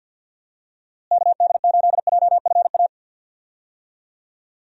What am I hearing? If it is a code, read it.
Morse code, heard as a keyed tone, about 52 wpm, 700 Hz. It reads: KD8JFA
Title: USA Call Signs - 6 Characters - Repeated 3x - 50wpm